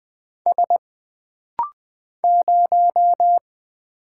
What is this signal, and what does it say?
Morse code, heard as a keyed tone, about 20 wpm, 700 Hz. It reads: S 0
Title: Single Letter-Number - Rapid-Fire 20wpm